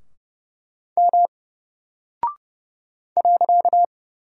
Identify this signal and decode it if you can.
Morse code, heard as a keyed tone, about 28 wpm, 700 Hz. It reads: M .
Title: Single Character (Letter:Number:Punctuation:Prosign) 30wpm